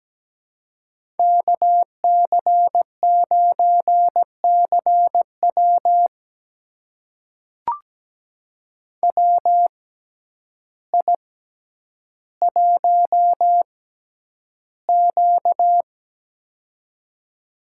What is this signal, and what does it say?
Morse code, heard as a keyed tone, about 17 wpm, spaced as for 9 wpm, 700 Hz. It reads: KC9CW WI1Q
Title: USA Call Signs - 6x Character Spacing - 17wpm